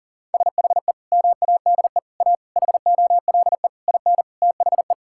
Morse code, heard as a keyed tone, about 40 wpm, 700 Hz. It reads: SHE MADE A HOLE IN THE
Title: Sentences from Top 1600 Words 40wpm